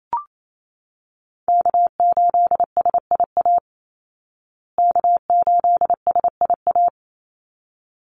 Morse code, heard as a keyed tone, about 28 wpm, 700 Hz. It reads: K8SIA K8SIA
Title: Call Signs Repeated 3x - 28wpm